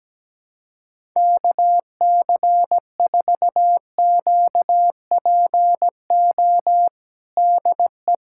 Morse code, heard as a keyed tone, about 17 wpm, 700 Hz. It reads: KC4QPO DE
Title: USA Callsign DE Callsign 17wpm